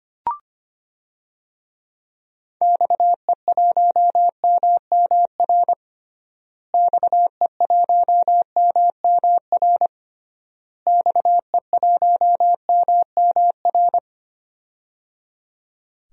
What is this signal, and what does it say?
Morse code, heard as a keyed tone, about 25 wpm, 700 Hz. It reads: XE1MMR XE1MMR XE1MMR
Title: DXCC-100 Call Signs - 6 Characters - Repeated 3x - 25wpm